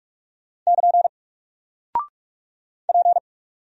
Morse code, heard as a keyed tone, about 44 wpm, 700 Hz. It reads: <KN> P
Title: Single Character (Letter:Number:Punctuation:Prosign) 45wpm